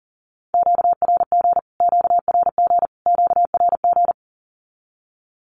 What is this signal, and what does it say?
Morse code, heard as a keyed tone, about 40 wpm, 700 Hz. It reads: QRG QRG QRG
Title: Q-Codes Repeated 3x 40wpm